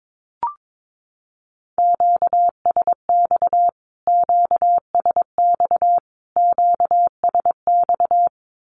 Morse code, heard as a keyed tone, about 22 wpm, 700 Hz. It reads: QSX QSX QSX
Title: Q-Codes Repeated 3x 22wpm